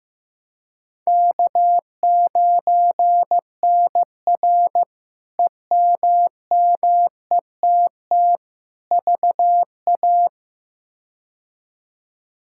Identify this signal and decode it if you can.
Morse code, heard as a keyed tone, about 16 wpm, 700 Hz. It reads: K9NR EMMETT VA
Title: K1USN SST QSO Practice - 15wpm